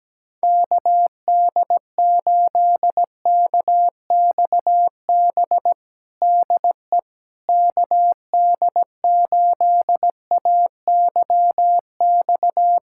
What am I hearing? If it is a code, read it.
Morse code, heard as a keyed tone, about 17 wpm, 700 Hz. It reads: KD8KXB DE KD8AYX
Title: USA Callsign DE Callsign 17wpm